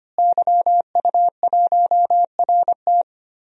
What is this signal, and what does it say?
Morse code, heard as a keyed tone, about 25 wpm, 700 Hz. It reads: YU1RT
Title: DXCC-100 Call Signs - 5 Characters - Repeated 3x - 25wpm